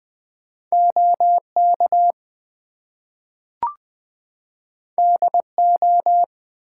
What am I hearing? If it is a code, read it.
Morse code, heard as a keyed tone, about 20 wpm, 700 Hz. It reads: OK DO
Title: Two Letter Words - 20wpm